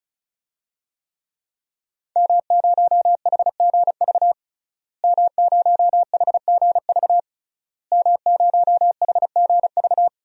Morse code, heard as a keyed tone, about 34 wpm, 700 Hz. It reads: M0HGV M0HGV M0HGV
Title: DXCC-100 Call Signs - 5 Characters - Repeated 3x - 35wpm